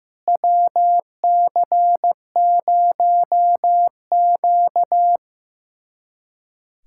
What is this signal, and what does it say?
Morse code, heard as a keyed tone, about 15 wpm, 700 Hz. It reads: WC0Q
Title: USA Callsign DE Callsign - 4 Characters - 15wpm